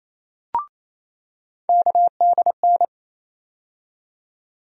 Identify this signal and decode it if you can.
Morse code, heard as a keyed tone, about 28 wpm, 700 Hz. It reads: KDN
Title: USA Call Sign Postfixes - 3 Letters 28wpm